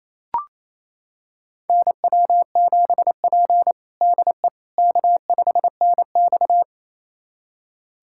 Morse code, heard as a keyed tone, about 28 wpm, 700 Hz. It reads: NW7P DE K5NX
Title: USA Callsign DE Callsign - 4 Characters - 28wpm